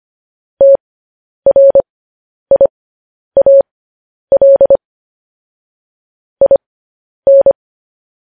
Morse code, heard as a keyed tone, about 25 wpm, spaced as for 10 wpm, 550 Hz. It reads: TRIAL IN